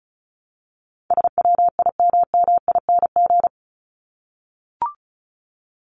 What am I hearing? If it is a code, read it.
Morse code, heard as a keyed tone, about 34 wpm, 700 Hz. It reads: SWIMMING
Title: Top 1501-1750 Words 35wpm